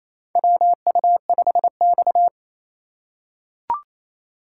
Morse code, heard as a keyed tone, about 28 wpm, 700 Hz. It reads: WU5X
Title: USA Call Signs - 2x Character Spacing - 28wpm